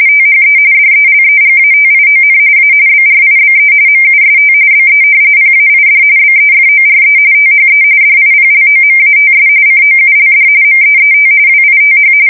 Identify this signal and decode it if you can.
RTTY, about 45.45 baud. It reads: THE DELEGATION A BRIBE MUST BE PAID BEFORE THE DIPLOMATS COULD MEET WITH
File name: fsk170 / usnavy1